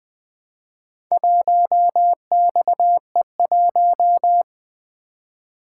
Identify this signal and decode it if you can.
Morse code, heard as a keyed tone, about 20 wpm, 700 Hz. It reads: 1XE1
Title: Four Letter-Number Combinations - 20wpm